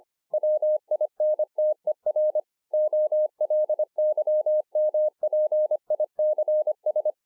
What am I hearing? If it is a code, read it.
Morse code, heard as a keyed tone, about 24 wpm, 610 Hz. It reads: WINTER OLYMPICS